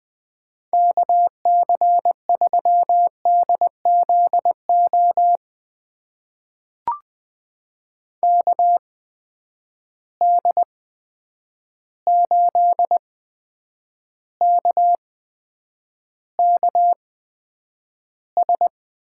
Morse code, heard as a keyed tone, about 20 wpm, spaced as for 9 wpm, 700 Hz. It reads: KC3DZO KD8KKS
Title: USA Call Signs - 6 Characters - 8x Character Spacing - 20wpm